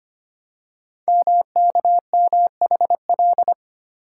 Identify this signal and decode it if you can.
Morse code, heard as a keyed tone, about 25 wpm, 700 Hz. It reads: MKMHL
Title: Code Groups - 5 Letters - 25wpm